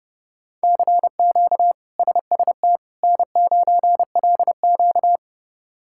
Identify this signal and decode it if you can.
Morse code, heard as a keyed tone, about 30 wpm, 700 Hz. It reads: CQ SST N9LQ
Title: K1USN SST QSO Practice - 30wpm